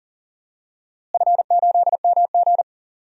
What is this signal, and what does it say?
Morse code, heard as a keyed tone, about 40 wpm, 700 Hz. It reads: F8MG
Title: DXCC-100 Call Signs - 4 Characters - Repeated 3x - 40wpm